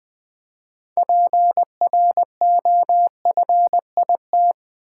Morse code, heard as a keyed tone, about 20 wpm, 700 Hz. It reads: PROFIT
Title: Six Letter Words - 20wpm